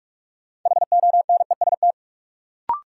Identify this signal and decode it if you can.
Morse code, heard as a keyed tone, about 46 wpm, 700 Hz. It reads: HONEST
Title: Sets of 5 Words Encore - New Word Drill - 45wpm